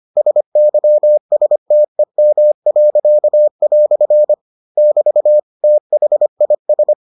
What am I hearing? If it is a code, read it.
Morse code, heard as a keyed tone, about 25 wpm, 600 Hz. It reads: SYSTEM." <BT> THIS